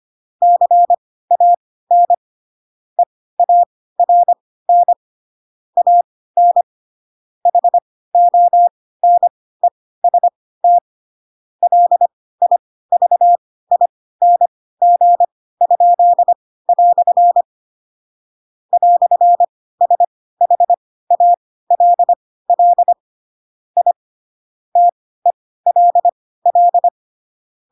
Morse code, heard as a keyed tone, about 25 wpm, spaced as for 16 wpm, 700 Hz. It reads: CAN EARN AN HONEST LIVING?" "SHALL I TELL
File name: Pinocchio / Pinocchio ch4 10 25 ik1hge